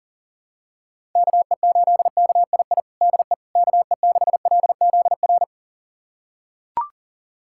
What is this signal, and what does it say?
Morse code, heard as a keyed tone, about 40 wpm, 700 Hz. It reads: KE8KII DE KE6LZR
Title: USA Callsign DE Callsign - 6 Characters - 40wpm